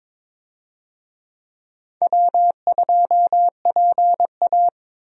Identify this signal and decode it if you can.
Morse code, heard as a keyed tone, about 22 wpm, 700 Hz. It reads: W2PA
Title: USA Call Signs - 4 Characters - 22wpm